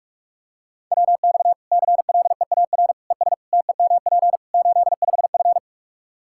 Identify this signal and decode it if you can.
Morse code, heard as a keyed tone, about 41 wpm, 700 Hz. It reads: WX CLEAR ES TEMP 85F
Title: Practice QSO - Exchange 2 - Info - 45wpm